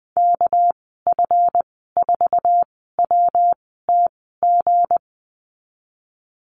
Morse code, heard as a keyed tone, about 20 wpm, spaced as for 14 wpm, 700 Hz. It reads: KF4WTG
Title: 231 Lesson 37g - Letter x - Call Signs 20wpm